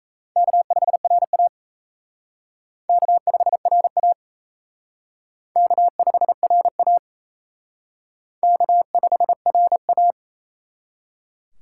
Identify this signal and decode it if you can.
Morse code, heard as a keyed tone, about 37 wpm, 700 Hz. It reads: K5RA K5RA K5RA K5RA